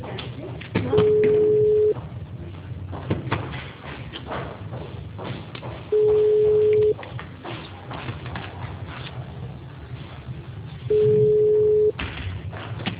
Ringback tone (Europe, 425 Hz).
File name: out- -2012- - - .6378